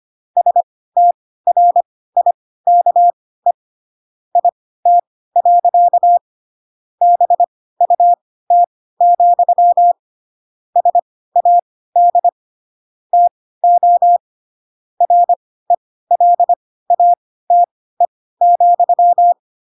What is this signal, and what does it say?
Morse code, heard as a keyed tone, about 25 wpm, spaced as for 16 wpm, 700 Hz. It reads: STRIKE IT. BUT, SAD TO RELATE,
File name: Pinocchio / Pinocchio ch4 10 25 ik1hge